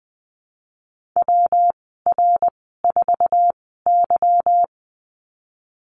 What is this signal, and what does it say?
Morse code, heard as a keyed tone, about 20 wpm, spaced as for 15 wpm, 700 Hz. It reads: WR4Y
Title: 186 Lesson 30g - Letter b - Call Signs 20wpm